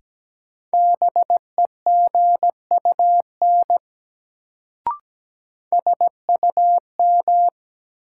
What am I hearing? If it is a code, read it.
Morse code, heard as a keyed tone, about 17 wpm, 700 Hz. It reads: BEGUN SUM
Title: Sentences from Top 1000 Words - Review Part 6 - New Word Drill - 17wpm